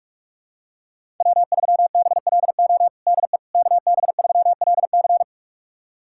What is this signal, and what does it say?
Morse code, heard as a keyed tone, about 46 wpm, 700 Hz. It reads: W3BLO DE KB3LC
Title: USA Callsign DE Callsign - 5 Characters - 45wpm